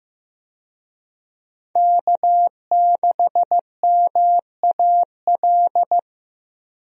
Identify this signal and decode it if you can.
Morse code, heard as a keyed tone, about 15 wpm, 700 Hz. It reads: K6MAL
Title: USA Call Signs - 5 Characters - Repeated 3x - 15wpm